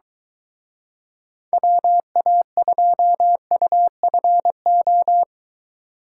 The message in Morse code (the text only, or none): WA2UFO